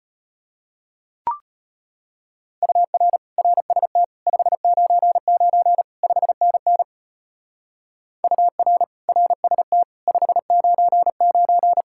Morse code, heard as a keyed tone, about 36 wpm, 700 Hz. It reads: UR RST 599 5NN UR RST 599